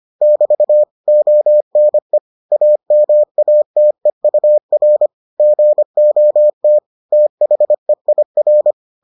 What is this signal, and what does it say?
Morse code, heard as a keyed tone, about 25 wpm, 600 Hz. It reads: <BT> ONE AMATEUR GOT THEIR